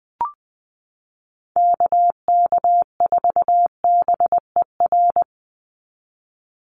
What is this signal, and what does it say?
Morse code, heard as a keyed tone, about 20 wpm, 700 Hz. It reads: KK4BER
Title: USA Call Signs 20wpm